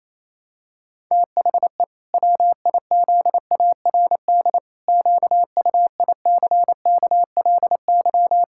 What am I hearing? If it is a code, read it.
Morse code, heard as a keyed tone, about 28 wpm, 700 Hz. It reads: THE WIZARD QUICKLY
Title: Pangram Challenge v2 - 28wpm